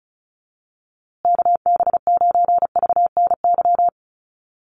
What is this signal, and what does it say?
Morse code, heard as a keyed tone, about 34 wpm, 700 Hz. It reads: KB9VNY
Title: USA Call Signs Repeated 3x 35wpm